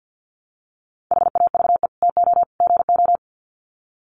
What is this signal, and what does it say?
Morse code, heard as a keyed tone, about 50 wpm, 700 Hz. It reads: HAVE TO GO
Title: Sets of 3 Words 50wpm